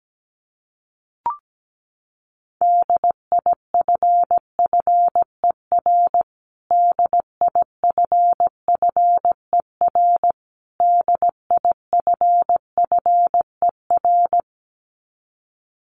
Morse code, heard as a keyed tone, about 17 wpm, 700 Hz. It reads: DIFFER DIFFER DIFFER
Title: Top 300 Words Repeated 17wpm